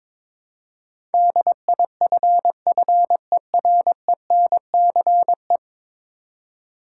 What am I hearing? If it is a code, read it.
Morse code, heard as a keyed tone, about 22 wpm, 700 Hz. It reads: DIFFERENCE
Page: morse-code-ninja-practice-22wpm